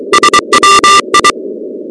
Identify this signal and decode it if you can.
Morse code, heard as a keyed tone, about 23 wpm, 1190 Hz, with noise. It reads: SWI
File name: contests / 2019-cw-wpx-kt5j-k5pi- -2330 0-16